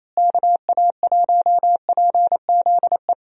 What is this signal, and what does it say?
Morse code, heard as a keyed tone, about 27 wpm, 700 Hz. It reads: KA1PZE
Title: USA Call Signs - 6 Characters - Repeated 3x - 28wpm